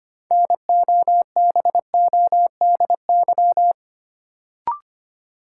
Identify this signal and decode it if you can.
Morse code, heard as a keyed tone, about 25 wpm, 700 Hz. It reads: NOBODY
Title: Sentences from Top 1600 Words - New Word Drill Alt - 25wpm